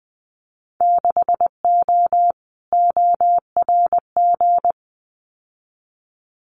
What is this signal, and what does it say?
Morse code, heard as a keyed tone, about 20 wpm, 700 Hz. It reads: 6O ORG
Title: Winter Field Day Exchanges Encore 20wpm